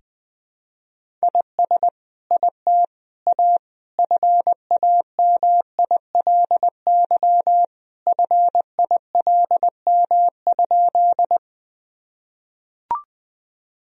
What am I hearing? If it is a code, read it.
Morse code, heard as a keyed tone, about 21 wpm, 700 Hz. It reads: IS IT A FAMILY FILM?
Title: Sentences from Top 1000 Words - Review Part 3 - 20wpm